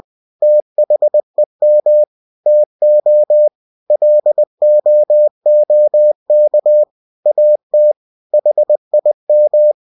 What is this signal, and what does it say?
Morse code, heard as a keyed tone, about 20 wpm, 600 Hz. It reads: THEM TO LOOK AT HIM
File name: cw-books / winnie-the-pooh / winnie-20wpm-page-0066